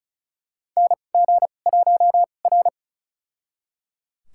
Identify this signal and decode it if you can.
Morse code, heard as a keyed tone, about 36 wpm, spaced as for 26 wpm, 700 Hz. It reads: NG1R
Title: USA Call Signs - 4 Characters - 2x Character Spacing - 35wpm